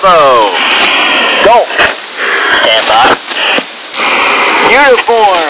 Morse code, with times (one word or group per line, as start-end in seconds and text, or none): none